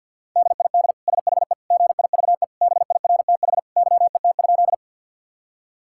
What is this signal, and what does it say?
Morse code, heard as a keyed tone, about 50 wpm, 700 Hz. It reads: DID SHE GIVE BIRTH YET?
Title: Sentences from Top 1000 Words - Review Part 5 - 50wpm